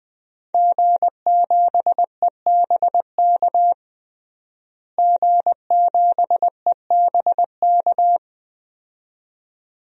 Morse code, heard as a keyed tone, about 21 wpm, 700 Hz. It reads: G7EBK G7EBK
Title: DXCC-100 Call Signs - 5 Characters - Repeated 2x - 20wpm